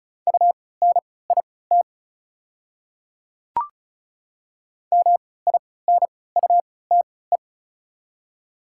Morse code, heard as a keyed tone, about 36 wpm, spaced as for 20 wpm, 690 Hz. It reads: UNIT MINUTE
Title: 3x Character Spacing - Top 500 Words - 35wpm